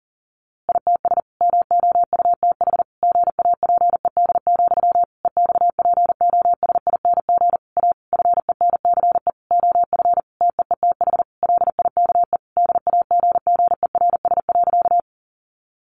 Morse code, heard as a keyed tone, about 40 wpm, 700 Hz. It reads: ITS MOUTH GAPED, EXPOSING A FENCE OF TEETH LIKE DAGGERS.
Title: A Sound Of Thunder 40wpm